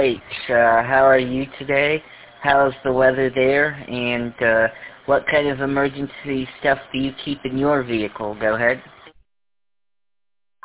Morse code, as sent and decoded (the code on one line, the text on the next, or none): none